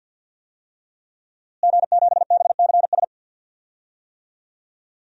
Morse code, heard as a keyed tone, about 52 wpm, 700 Hz. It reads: G7BCS
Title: DXCC-100 Call Signs - 5 Characters - Repeated 3x - 50wpm